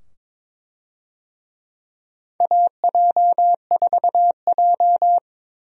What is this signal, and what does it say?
Morse code, heard as a keyed tone, about 21 wpm, 700 Hz. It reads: AJ4J
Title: USA Call Signs - 4 Characters - 22wpm